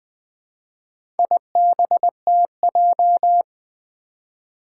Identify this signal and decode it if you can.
Morse code, heard as a keyed tone, about 20 wpm, 700 Hz. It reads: IBTJ
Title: Four Challenging Letter-Number Combinations 20wpm